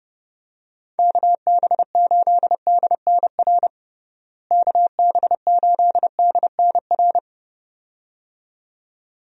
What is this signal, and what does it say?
Morse code, heard as a keyed tone, about 30 wpm, 700 Hz. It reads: KB8DNR KB8DNR
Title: USA Call Signs - 6 Characters - Repeated 2x - 30wpm